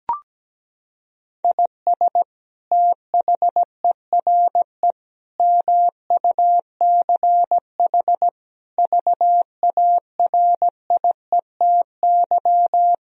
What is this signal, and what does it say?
Morse code, heard as a keyed tone, about 17 wpm, 700 Hz. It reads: IS THERE MUCH VARIETY
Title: Sentences from Top 1000 Words - Review Part 4 - 17wpm